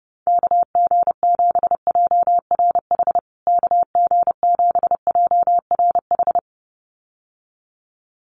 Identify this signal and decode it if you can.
Morse code, heard as a keyed tone, about 30 wpm, 700 Hz. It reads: KG7JRH KG7JRH
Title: USA Call Signs Repeated 2x 30wpm